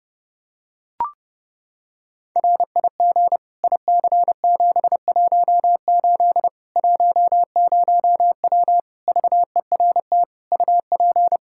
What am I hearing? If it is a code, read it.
Morse code, heard as a keyed tone, about 30 wpm, 700 Hz. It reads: RIG IC718 10W VERT UP
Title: Practice QSO - Exchange 2 - Info - 30wpm